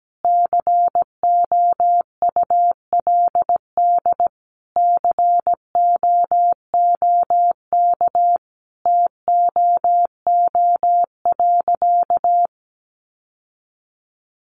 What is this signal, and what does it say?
Morse code, heard as a keyed tone, about 17 wpm, 700 Hz. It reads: COULD COOK TOO.
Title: Sentences from Top 700 Words 17wpm